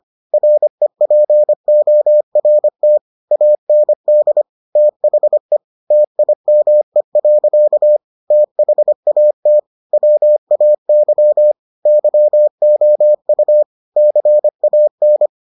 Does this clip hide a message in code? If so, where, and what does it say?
Yes: Morse code from 0.3 to 15.3 s, heard as a keyed tone, about 25 wpm, 600 Hz. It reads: REPORT AND THE TIME. THAT WAY YOU CAN